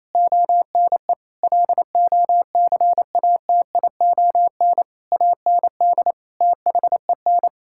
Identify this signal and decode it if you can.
Morse code, heard as a keyed tone, about 28 wpm, 700 Hz. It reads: ONE LOCATION AND THEN
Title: Sentences from Top 1000 Words - Review Part 6 - 28wpm